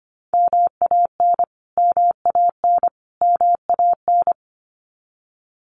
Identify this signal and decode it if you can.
Morse code, heard as a keyed tone, about 25 wpm, 700 Hz. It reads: MAN MAN MAN
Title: Top 200 Words Repeated 25wpm